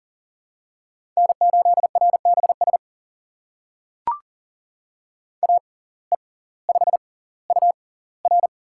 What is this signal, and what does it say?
Morse code, heard as a keyed tone, about 40 wpm, spaced as for 21 wpm, 690 Hz. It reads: N8RBS AE5UR
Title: USA Call Signs - 5 Characters - 6x Character Spacing - 40wpm